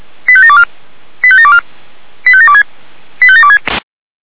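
Five-tone selective call (ZVEI), heard as quick runs of five tones, tones of about 70 ms. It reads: 85724 85724 85726 85726
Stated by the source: Five-tone ZVEI selective call received on SDR.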